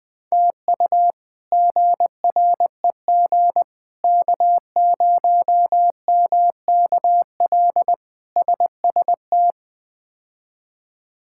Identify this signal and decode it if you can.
Morse code, heard as a keyed tone, about 20 wpm, 700 Hz. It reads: TU GREG K0MKL SST
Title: K1USN SST QSO Practice - 20wpm